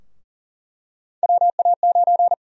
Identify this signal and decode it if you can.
Morse code, heard as a keyed tone, about 40 wpm, 700 Hz. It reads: WA9
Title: USA Call Sign Prefixes - Repeated 2x - 40wpm